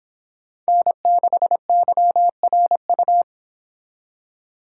Morse code, heard as a keyed tone, about 26 wpm, 700 Hz. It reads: N6YRU